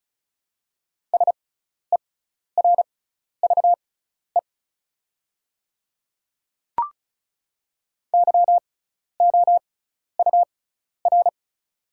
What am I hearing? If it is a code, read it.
Morse code, heard as a keyed tone, about 36 wpm, spaced as for 12 wpm, 700 Hz. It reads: SERVE YOUR
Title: 6x Character Spacing - Top 500 Words - 35wpm